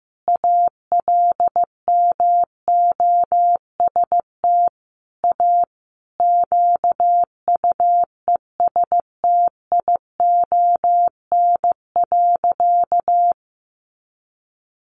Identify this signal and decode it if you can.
Morse code, heard as a keyed tone, about 15 wpm, 700 Hz. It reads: ALMOST A QUESTION.
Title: Sentences from Top 1200 Words 15wpm